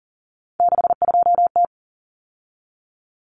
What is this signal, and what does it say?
Morse code, heard as a keyed tone, about 40 wpm, 700 Hz. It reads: 62T
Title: Common Cut Numbers - 3-Digit 40wpm